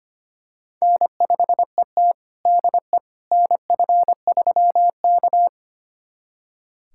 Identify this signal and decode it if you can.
Morse code, heard as a keyed tone, about 25 wpm, 700 Hz. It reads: N5ET DE NF3K
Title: USA Callsign DE Callsign - 4 Characters - 25wpm